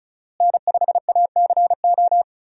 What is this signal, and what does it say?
Morse code, heard as a keyed tone, about 34 wpm, 700 Hz. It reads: N5ACO
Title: USA Call Signs - 5 Characters - Repeated 3x - 35wpm